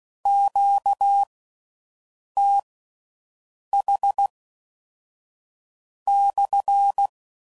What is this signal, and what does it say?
Morse code, heard as a keyed tone, about 16 wpm, spaced as for 5 wpm, 800 Hz. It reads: QTH/